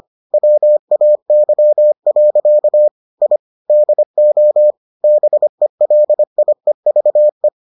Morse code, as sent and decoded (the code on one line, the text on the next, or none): .-- .- -.-- .-.-.- / .. / -.. --- / -... .-.. .. ...- .
WAY. I DO BELIEVE